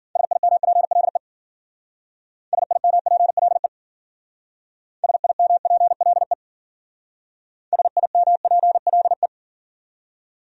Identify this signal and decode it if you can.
Morse code, heard as a keyed tone, about 51 wpm, 700 Hz. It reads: SIMPLE SIMPLE SIMPLE SIMPLE